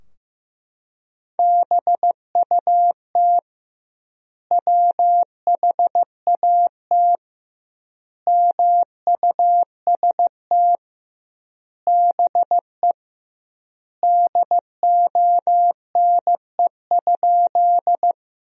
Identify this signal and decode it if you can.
Morse code, heard as a keyed tone, about 15 wpm, 700 Hz. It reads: BUT WHAT MUST BE DONE?
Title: 2x Word-Spacing - Sentences from Top 500 Words - 15wpm